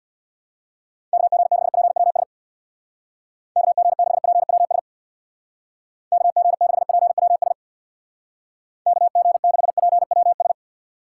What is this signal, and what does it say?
Morse code, heard as a keyed tone, about 64 wpm, 700 Hz. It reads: KG6PWS KG6PWS KG6PWS KG6PWS